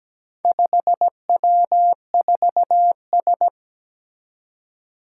Morse code, heard as a keyed tone, about 17 wpm, 700 Hz. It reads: 5W4S
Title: Four Challenging Letter-Number Combinations 17wpm